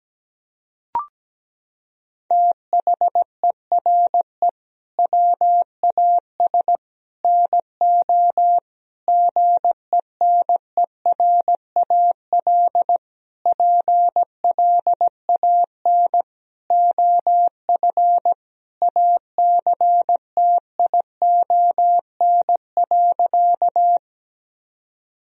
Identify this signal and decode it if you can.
Morse code, heard as a keyed tone, about 17 wpm, 700 Hz. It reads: THERE WAS NO GENERAL PLAN OF ACTION.
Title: Sentences from Top 1000 Words - Review Part 4 - 17wpm